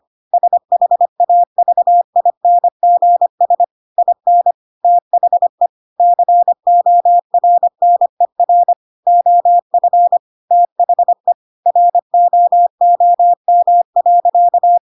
Morse code, heard as a keyed tone, about 25 wpm, 700 Hz. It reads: SHAVINGS IN THE CORNER OF THE ROOM.
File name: Pinocchio / Pinocchio ch7 25 25 ik1hge